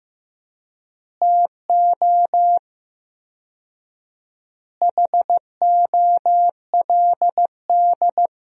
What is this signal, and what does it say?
Morse code, heard as a keyed tone, about 15 wpm, 700 Hz. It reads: TO HOLD
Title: 4x Word-Spacing - Sentences from Top 400 Words - 15wpm